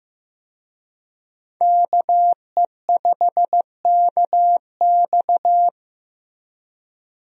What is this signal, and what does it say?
Morse code, heard as a keyed tone, about 15 wpm, 700 Hz. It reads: KE5KX
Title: USA Call Signs - 5 Characters - Repeated 3x - 15wpm